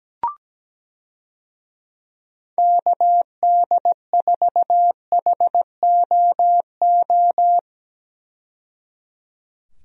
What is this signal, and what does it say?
Morse code, heard as a keyed tone, about 17 wpm, 700 Hz. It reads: KD4HOO
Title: USA Call Signs - 6 Characters - 17wpm